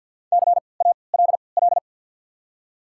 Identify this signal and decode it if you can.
Morse code, heard as a keyed tone, about 53 wpm, spaced as for 29 wpm, 700 Hz. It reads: CALL
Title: 3x Character Spacing - Top 500 Words - 50wpm